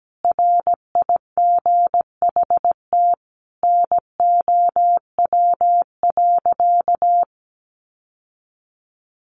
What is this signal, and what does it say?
Morse code, heard as a keyed tone, about 17 wpm, 700 Hz. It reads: RIGHT NOW.
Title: Sentences from Top 700 Words 17wpm